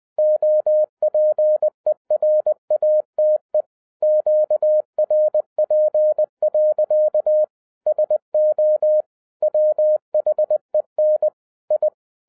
Morse code, heard as a keyed tone, about 20 wpm, 610 Hz. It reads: OPERATE QRP. SO WHEN I
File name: 2011-10-mp3 / n13-TIPS FROM MY 34 YEARS OF EXPERIENCE 20wpm